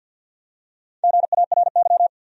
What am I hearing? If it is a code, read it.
Morse code, heard as a keyed tone, about 52 wpm, 700 Hz. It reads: GARY
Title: USA Common Names - Max 4 Letters - 50wpm